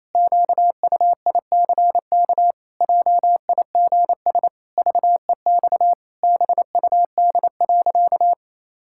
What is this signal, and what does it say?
Morse code, heard as a keyed tone, about 28 wpm, 700 Hz. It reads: QUICK JIGS VEX BUD.
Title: Pangram Challenge v2 - 28wpm